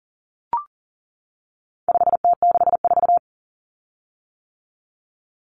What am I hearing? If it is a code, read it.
Morse code, heard as a keyed tone, about 40 wpm, 700 Hz. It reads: 5T64
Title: Common Cut Numbers - 4-Digit 40wpm